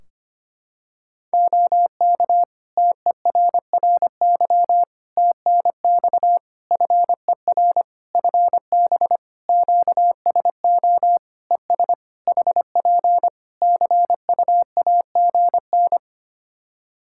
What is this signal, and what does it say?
Morse code, heard as a keyed tone, about 25 wpm, 700 Hz. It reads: OK TERRY TNX FER FB QSO ES HP CUAGN
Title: Practice QSO - Exchange 4 - Ending - 25wpm